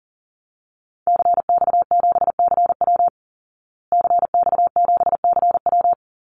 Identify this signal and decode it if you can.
Morse code, heard as a keyed tone, about 40 wpm, 700 Hz. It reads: CX7CW CX7CW
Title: Call Signs Repeated 3x - 40wpm